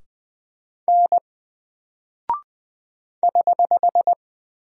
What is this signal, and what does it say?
Morse code, heard as a keyed tone, about 20 wpm, 700 Hz. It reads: N <HH>
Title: Single Character (Letter:Number:Punctuation:Prosign) 20wpm